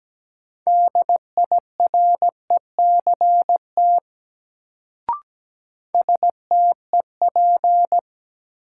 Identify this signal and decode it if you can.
Morse code, heard as a keyed tone, about 17 wpm, 700 Hz. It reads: DIRECT STEP
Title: Sentences from Top 400 Words - New Word Drill - 17wpm